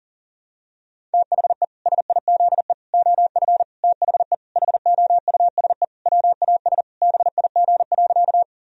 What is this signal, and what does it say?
Morse code, heard as a keyed tone, about 40 wpm, 700 Hz. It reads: THE SIZE OF THE HOUSE WAS BIG.
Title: Sentences from Top 1000 Words - Review Part 2 - 40wpm